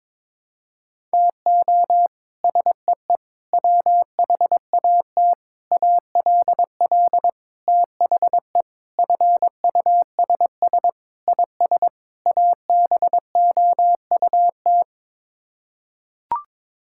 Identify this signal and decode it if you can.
Morse code, heard as a keyed tone, about 22 wpm, 700 Hz. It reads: TO SEE WHAT ALL THE FUSS IS ABOUT
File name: practice / 22wpm / Sets of 8 Words - 22wpm